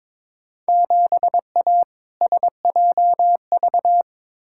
Morse code, heard as a keyed tone, about 22 wpm, 700 Hz. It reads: 7A SJV
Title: Field Day Exchanges Encore - 22wpm